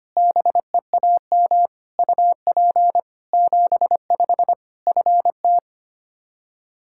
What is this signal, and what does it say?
Morse code, heard as a keyed tone, about 25 wpm, 700 Hz. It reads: BEAM UP 75 FT
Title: Practice QSO - Exchange 2 - Info - 25wpm